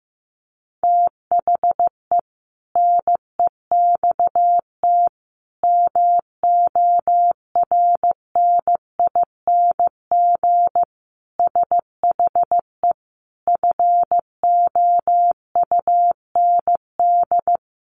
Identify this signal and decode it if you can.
Morse code, heard as a keyed tone, about 14 wpm, 700 Hz. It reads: THE NEXT MORNING SHE FOUND
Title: Sentences from Top 1200 Words 15wpm